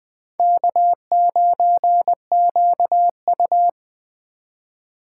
Morse code, heard as a keyed tone, about 23 wpm, 700 Hz. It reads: K9QU
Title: Speed Racing - USA 1x2 Call Signs 20wpm